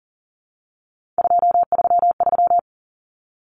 Morse code, heard as a keyed tone, about 40 wpm, 700 Hz. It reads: 233
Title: Numbers - 3-Digit 40wpm